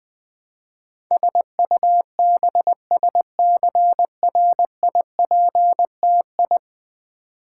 Morse code, heard as a keyed tone, about 20 wpm, 700 Hz. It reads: SUBSCRIPTI
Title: Build A Word - 5 Letters At A Time - 12-Letter Words - 20wpm